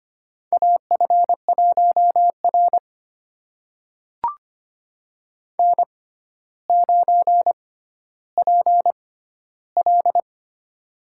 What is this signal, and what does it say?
Morse code, heard as a keyed tone, about 25 wpm, spaced as for 13 wpm, 700 Hz. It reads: AF1R N9PL
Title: USA Call Signs - 4 Characters - 6x Character Spacing - 25wpm